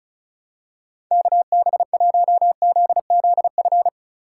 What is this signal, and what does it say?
Morse code, heard as a keyed tone, about 35 wpm, 700 Hz. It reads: KB1ZZF
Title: USA Call Signs - 4x Character Spacing - 35wpm